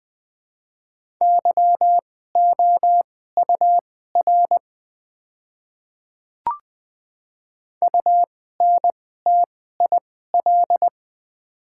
Morse code, heard as a keyed tone, about 20 wpm, spaced as for 14 wpm, 700 Hz. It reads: YOUR UNTIL
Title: 2x Character Spacing - Top 300 Words - 20wpm